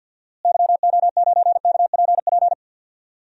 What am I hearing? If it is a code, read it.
Morse code, heard as a keyed tone, about 47 wpm, 700 Hz. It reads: YO9KPP
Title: DXCC-100 Call Signs - 6 Characters - Repeated 2x - 50wpm